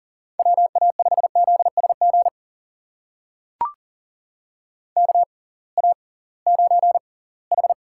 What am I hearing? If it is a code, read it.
Morse code, heard as a keyed tone, about 40 wpm, spaced as for 23 wpm, 700 Hz. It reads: WA5ZSG KA9H
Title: USA Call Signs - 6 Characters - 6x Character Spacing - 40wpm